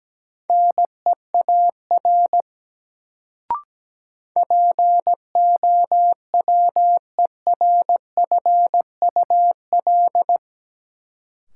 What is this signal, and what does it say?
Morse code, heard as a keyed tone, about 17 wpm, 700 Hz. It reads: NEAR POWERFUL
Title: Sets of 5 Words Encore - New Word Drill - 17wpm